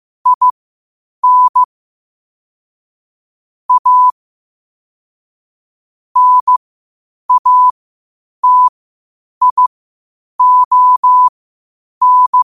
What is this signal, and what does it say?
Morse code, heard as a keyed tone, about 15 wpm, spaced as for 8 wpm, 1000 Hz. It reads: IN A NATION
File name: qotdfiles / 8wpm